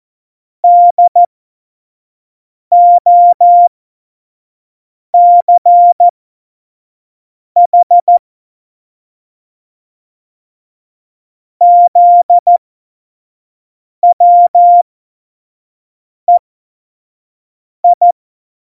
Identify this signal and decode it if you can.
Morse code, heard as a keyed tone, about 14 wpm, spaced as for 5 wpm, 700 Hz. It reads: DOCH ZWEI
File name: Morsekurs / Witze / Witz 22. Januar 2019, 25 F70